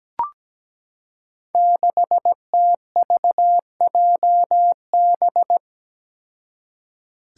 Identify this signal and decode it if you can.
Morse code, heard as a keyed tone, about 17 wpm, 700 Hz. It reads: 6TVJB
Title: Five Challenging Letter-Number Combinations 17wpm